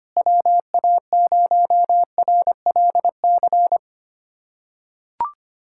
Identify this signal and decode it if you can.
Morse code, heard as a keyed tone, about 25 wpm, 700 Hz. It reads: WA0RLC
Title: USA Callsign DE Callsign 25wpm